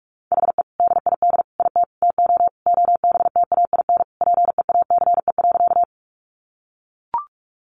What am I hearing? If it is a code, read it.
Morse code, heard as a keyed tone, about 44 wpm, 700 Hz. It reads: HE DID IT TO OBTAIN PEACE.
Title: Sentences from Top 1500 Words 45wpm